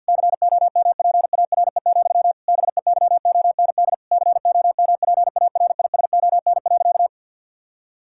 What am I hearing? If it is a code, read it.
Morse code, heard as a keyed tone, about 50 wpm, 700 Hz. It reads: COMPARE, BEYOND COMPARISON.
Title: Sentences from Top 1600 Words 50wpm